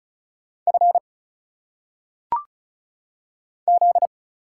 Morse code, heard as a keyed tone, about 35 wpm, 700 Hz. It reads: F Z
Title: Single Letters - Phonetic Pronunciation 35wpm